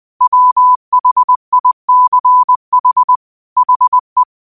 Morse code, heard as a keyed tone, about 20 wpm, 1000 Hz. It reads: WHICH HE
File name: qotdfiles / 20wpm